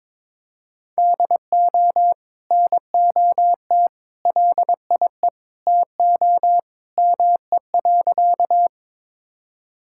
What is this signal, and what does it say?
Morse code, heard as a keyed tone, about 22 wpm, 700 Hz. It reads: DO NOT LIE TO ME.
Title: Sentences from Top 1600 Words 22wpm